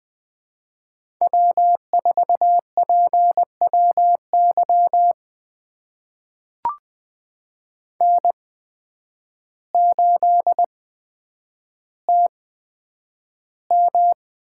Morse code, heard as a keyed tone, about 20 wpm, spaced as for 9 wpm, 700 Hz. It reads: W4PWY N8TM
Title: USA Call Signs - 8x Character Spacing - 20wpm